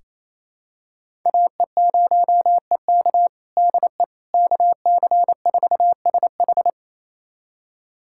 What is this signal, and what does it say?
Morse code, heard as a keyed tone, about 28 wpm, 700 Hz. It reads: AE0EK DE KC4SH
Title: USA Callsign DE Callsign - 5 Characters - 28wpm